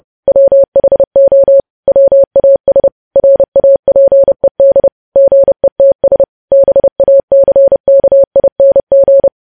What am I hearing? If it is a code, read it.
Morse code, heard as a keyed tone, about 30 wpm, 550 Hz. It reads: WHO WAS RAPED GETS BACKING
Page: news headlines in morse code at 30 wpm